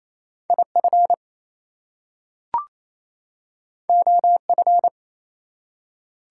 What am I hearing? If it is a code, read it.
Morse code, heard as a keyed tone, about 28 wpm, 700 Hz. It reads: IF OF
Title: Two Letter Words - 28wpm